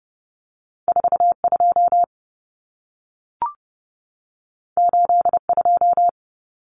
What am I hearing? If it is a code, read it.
Morse code, heard as a keyed tone, about 30 wpm, 700 Hz. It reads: 42 82
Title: Common Cut Numbers - 2-Digit 30wpm